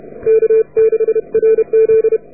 Morse code, heard as a keyed tone, about 28 wpm, 440 Hz. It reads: K6RZ